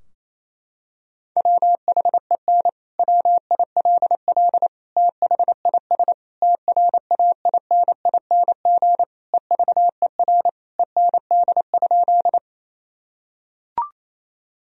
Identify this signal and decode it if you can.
Morse code, heard as a keyed tone, about 28 wpm, 700 Hz. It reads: WHEN WILL THIS TRAINING EVER END?
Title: Sentences from Top 1000 Words - Review Part 4 - 28wpm